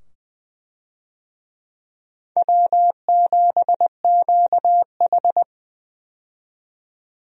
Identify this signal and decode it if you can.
Morse code, heard as a keyed tone, about 21 wpm, 700 Hz. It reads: W7QH
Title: USA Call Signs - 4 Characters - 20wpm